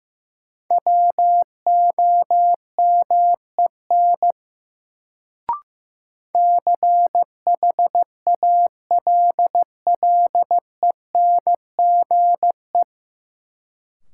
Morse code, heard as a keyed tone, about 15 wpm, 700 Hz. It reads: WOMEN CHALLENGE
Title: Sentences from Top 1300 Words - New Word Drill Alt - 15wpm